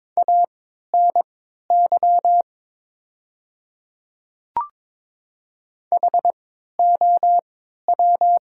Morse code, heard as a keyed tone, about 22 wpm, spaced as for 12 wpm, 700 Hz. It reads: ANY HOW
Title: 3x Character Spacing - Top 200 Words - 22wpm